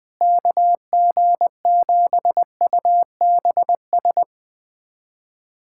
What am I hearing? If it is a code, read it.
Morse code, heard as a keyed tone, about 20 wpm, 700 Hz. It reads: KG7UBS
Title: USA Callsign DE Callsign 20wpm